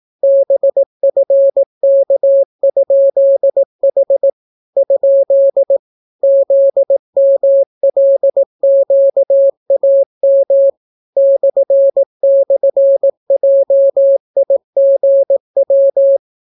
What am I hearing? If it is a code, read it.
Morse code, heard as a keyed tone, about 18 wpm, 550 Hz. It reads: BFK?H ? ZMLQAM //JIGW